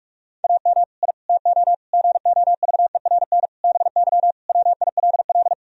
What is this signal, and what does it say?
Morse code, heard as a keyed tone, about 46 wpm, 700 Hz. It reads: AM I TO GOVERN BY WILL
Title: Sentences from Top 1000 Words - Review Part 2 - 45wpm